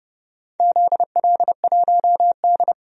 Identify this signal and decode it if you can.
Morse code, heard as a keyed tone, about 30 wpm, 700 Hz. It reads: ZL1D
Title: DXCC-100 Call Signs - 4 Characters - 30wpm